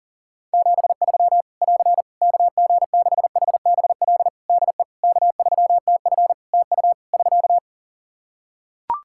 Morse code, heard as a keyed tone, about 40 wpm, 700 Hz. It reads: 73 <AR> KG6HBL DE K3TF TU <SK>
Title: Practice QSO - Exchange 4 - Ending - 40wpm